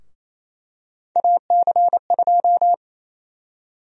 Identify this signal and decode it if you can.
Morse code, heard as a keyed tone, about 28 wpm, 700 Hz. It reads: AC2
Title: USA Call Sign Prefixes - Repeated 2x - 28wpm